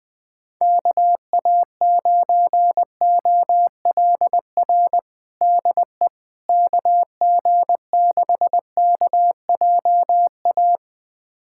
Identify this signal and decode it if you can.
Morse code, heard as a keyed tone, about 20 wpm, 700 Hz. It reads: KA9OLR DE KG6KJA
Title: USA Callsign DE Callsign - 6 Characters - 20wpm